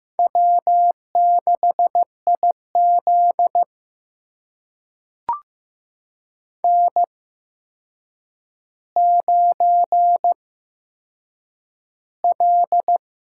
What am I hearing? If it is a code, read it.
Morse code, heard as a keyed tone, about 15 wpm, spaced as for 7 wpm, 700 Hz. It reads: W6IZ N9L
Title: USA Call Signs - 8x Character Spacing - 15wpm